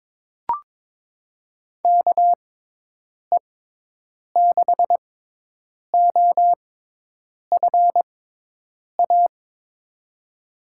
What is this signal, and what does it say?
Morse code, heard as a keyed tone, about 22 wpm, spaced as for 8 wpm, 700 Hz. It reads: KE6OFA
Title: USA Call Signs - 6 Characters - 6x Character Spacing - 22wpm